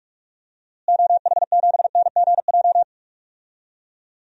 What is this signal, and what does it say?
Morse code, heard as a keyed tone, about 45 wpm, 700 Hz. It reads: OH7NGJ
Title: DXCC-100 Call Signs - 6 Characters - Repeated 3x - 45wpm